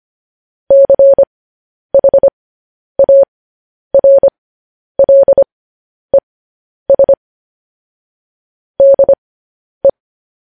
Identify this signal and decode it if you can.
Morse code, heard as a keyed tone, about 25 wpm, spaced as for 10 wpm, 550 Hz. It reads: CHARLES DE
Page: news headlines in morse code at 10 wpm